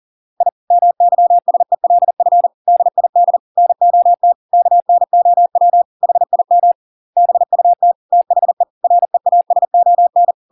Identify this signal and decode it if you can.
Morse code, heard as a keyed tone, about 40 wpm, 700 Hz. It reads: I MYSELF DID NOT KNOW HIM BUT THE REASON